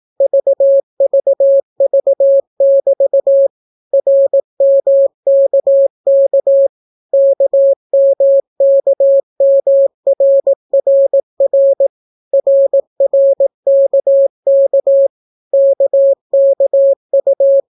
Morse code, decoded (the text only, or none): VVV<BT> RMKK KMKMRRR RRKK KKU